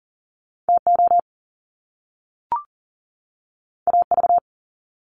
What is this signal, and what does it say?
Morse code, heard as a keyed tone, about 42 wpm, 700 Hz. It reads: TO AV
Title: Top Two Letter Combinations in Words 40wpm